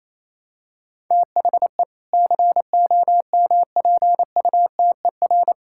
Morse code, heard as a keyed tone, about 27 wpm, 700 Hz. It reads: THE COMPUTER
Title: Sentences from Top 1000 Words - Review Part 6 - 28wpm